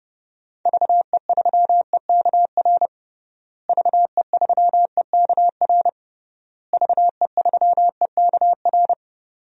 Morse code, heard as a keyed tone, about 32 wpm, 700 Hz. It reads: VE3EKR VE3EKR VE3EKR
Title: DXCC-100 Call Signs - 6 Characters - Repeated 3x - 30wpm